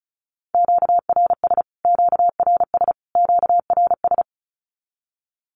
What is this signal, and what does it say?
Morse code, heard as a keyed tone, about 34 wpm, 700 Hz. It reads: QRS QRS QRS
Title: Q-Codes Repeated 3x 35wpm